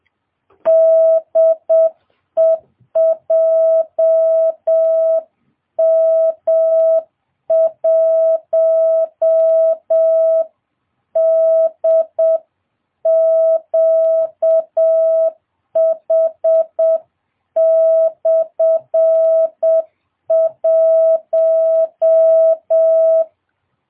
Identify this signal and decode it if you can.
Morse code, heard as a keyed tone, about 7 wpm, 650 Hz. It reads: DE JM1DQH/1